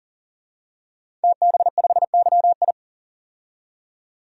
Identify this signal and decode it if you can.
Morse code, heard as a keyed tone, about 40 wpm, 700 Hz. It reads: TB5YI
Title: DXCC-100 Call Signs - 5 Characters - 40wpm